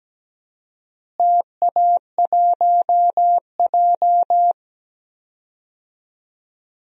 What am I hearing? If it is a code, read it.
Morse code, heard as a keyed tone, about 17 wpm, 700 Hz. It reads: TA1J
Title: DXCC-100 Call Signs - 4 Characters - Repeated 3x - 17wpm